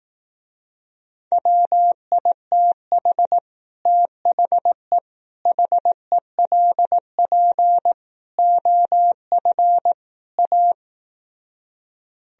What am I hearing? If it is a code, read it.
Morse code, heard as a keyed tone, about 18 wpm, 700 Hz. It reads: WITH THE HELP OF A
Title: Speed-Racing - Sets of 5 Words 15wpm